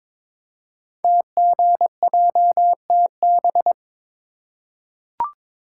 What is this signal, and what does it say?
Morse code, heard as a keyed tone, about 22 wpm, 700 Hz. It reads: TGJTB
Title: Code Groups - 5 Letters - 22wpm